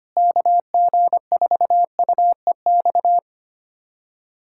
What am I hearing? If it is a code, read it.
Morse code, heard as a keyed tone, about 25 wpm, 700 Hz. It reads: KG4UEX
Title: USA Callsign DE Callsign - 6 Characters - 25wpm